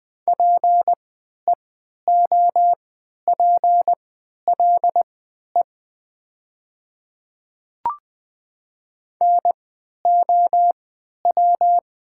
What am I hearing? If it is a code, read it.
Morse code, heard as a keyed tone, about 20 wpm, spaced as for 11 wpm, 700 Hz. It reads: PEOPLE NOW
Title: 3x Character Spacing - Top 100 Words - 20wpm